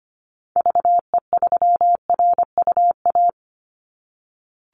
Morse code, heard as a keyed tone, about 25 wpm, 700 Hz. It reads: VE3RUA
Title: Call Signs - 25wpm